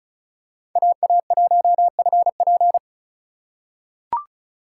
Morse code, heard as a keyed tone, about 36 wpm, 700 Hz. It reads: AA1FP
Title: USA Call Signs - 5 Characters - 2x Character Spacing - 35wpm